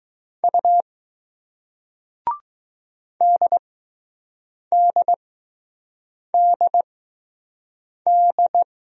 Morse code, heard as a keyed tone, about 21 wpm, spaced as for 6 wpm, 700 Hz. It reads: U DDDD